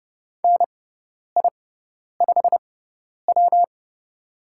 Morse code, heard as a keyed tone, about 30 wpm, spaced as for 10 wpm, 700 Hz. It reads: NI5W
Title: USA Call Signs - 4 Characters - 6x Character Spacing - 30wpm